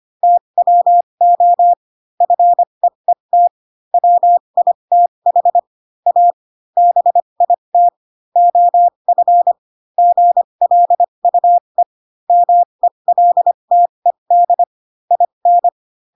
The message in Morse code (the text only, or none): TWO FEET WITH A BIT OF GLUE MELTED IN